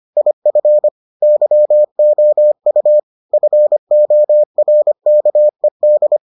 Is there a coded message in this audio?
Yes: Morse code from 0.2 to 6.2 s, heard as a keyed tone, about 25 wpm, 600 Hz. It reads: IF YOU FORKED